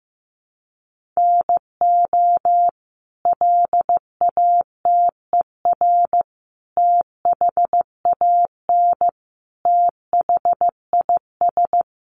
Morse code, heard as a keyed tone, about 15 wpm, 700 Hz. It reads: NO LATER THAN THIS
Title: Sentences from Top 700 Words 15wpm